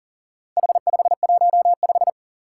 Morse code, heard as a keyed tone, about 42 wpm, 700 Hz. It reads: H515
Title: Four Challenging Letter-Number Combinations 40wpm